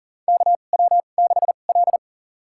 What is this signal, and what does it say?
Morse code, heard as a keyed tone, about 39 wpm, spaced as for 29 wpm, 700 Hz. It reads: KW6L
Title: USA Call Signs - 2x Character Spacing - 40wpm